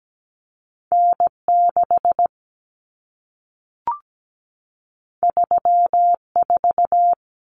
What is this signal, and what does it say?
Morse code, heard as a keyed tone, about 17 wpm, 700 Hz. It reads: N6 34
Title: Common Cut Numbers - 2-Digit 17wpm